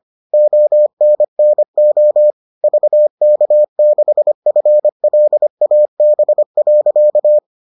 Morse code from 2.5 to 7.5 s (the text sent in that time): VK6FLAB.